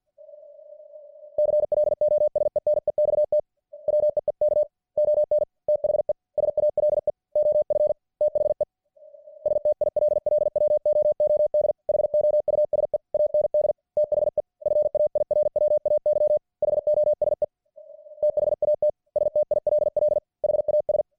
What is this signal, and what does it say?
Morse code, heard as a keyed tone, about 50 wpm, 600 Hz. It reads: CLOSE NEXT WEEK ON THE SALE OF THE STILLWOOD HOUSE AND THE FAIRWAY HOSE THAT STILL HAS